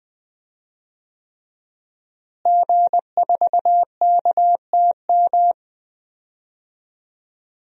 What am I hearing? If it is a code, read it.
Morse code, heard as a keyed tone, about 20 wpm, 700 Hz. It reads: G4KTM
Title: DXCC-100 Call Signs - 5 Characters - Repeated 3x - 20wpm